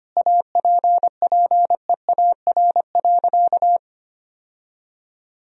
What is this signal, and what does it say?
Morse code, heard as a keyed tone, about 25 wpm, 700 Hz. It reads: APPEAR.
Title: Sentences from Top 1000 Words - Review Part 2 - 25wpm